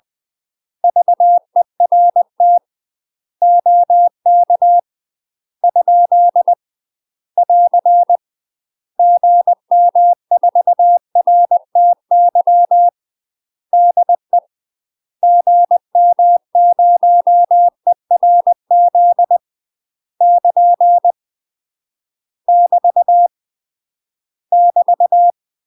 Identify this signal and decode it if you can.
Morse code, heard as a keyed tone, about 20 wpm, 700 Hz. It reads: VERT OK ? <AR> GM4RTY DE GM0ERZ <KN> <BT> <BT>
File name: UK MASTER QSOs TRAINING / 20WPM / UK MASTER QSOs 0043